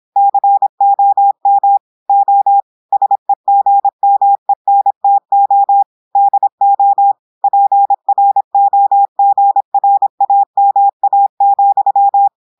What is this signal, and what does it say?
Morse code, heard as a keyed tone, about 26 wpm, 800 Hz. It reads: COM O SEGMENTO DO PROGRAMA,
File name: portugais texte en clair 2 vitesse 26 mots minutes